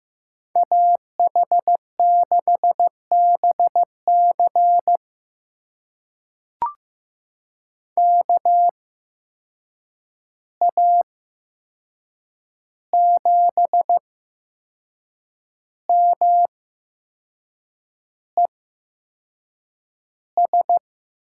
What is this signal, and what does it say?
Morse code, heard as a keyed tone, about 15 wpm, spaced as for 6 wpm, 700 Hz. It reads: AH6BC KA7MES
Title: USA Call Signs - 8x Character Spacing - 15wpm